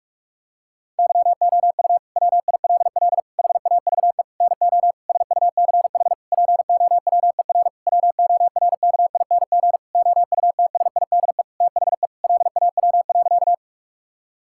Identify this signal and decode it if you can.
Morse code, heard as a keyed tone, about 46 wpm, 700 Hz. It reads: YOU WILL HAVE NO SUCH POWER WORKING OUTSIDE THE LAW.
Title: Sentences from Top 1000 Words - Review Part 6 - 45wpm